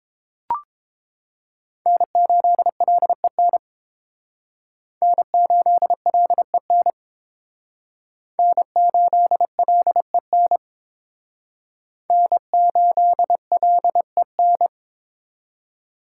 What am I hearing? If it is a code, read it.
Morse code, heard as a keyed tone, about 28 wpm, 700 Hz. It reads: N8LEN N8LEN N8LEN N8LEN